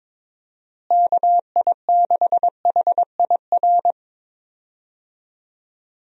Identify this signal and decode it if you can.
Morse code, heard as a keyed tone, about 22 wpm, 700 Hz. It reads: KI6HIR
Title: USA Call Signs - 6 Characters - Repeated 3x - 22wpm